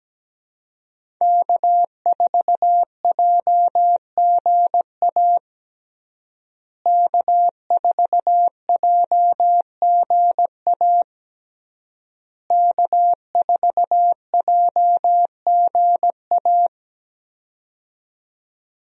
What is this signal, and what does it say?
Morse code, heard as a keyed tone, about 17 wpm, 700 Hz. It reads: K4JGA K4JGA K4JGA
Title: USA Call Signs - 5 Characters - Repeated 3x - 17wpm